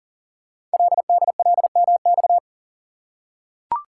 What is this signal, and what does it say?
Morse code, heard as a keyed tone, about 42 wpm, 690 Hz. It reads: LDLMX
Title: Code Groups - 5 Letters - 40wpm